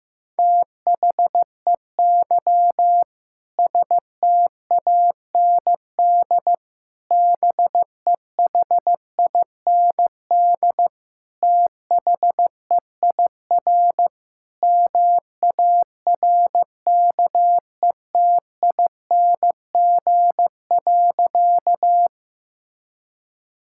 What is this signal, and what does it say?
Morse code, heard as a keyed tone, about 15 wpm, 700 Hz. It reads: THEY STAND BEHIND THEIR MARKETING.
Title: Sentences from Top 1000 Words - Review Part 5 - 15wpm